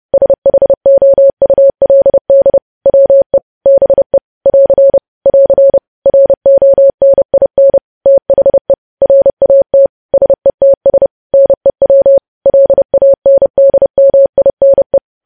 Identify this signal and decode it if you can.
Morse code, heard as a keyed tone, about 30 wpm, 550 Hz. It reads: SHOULD WE BE <AR> <AR> RONIN THE RAT SETS NEW LANDMINE
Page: news headlines in morse code at 30 wpm